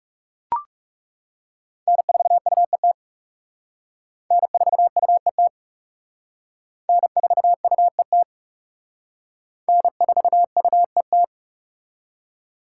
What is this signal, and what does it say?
Morse code, heard as a keyed tone, about 36 wpm, 700 Hz. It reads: N4UET N4UET N4UET N4UET
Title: Speed Racing - USA 1x3 Call Signs 30wpm